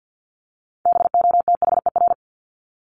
Morse code, heard as a keyed tone, about 50 wpm, 700 Hz. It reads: BOTHER
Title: Top 1501-1750 Words 50wpm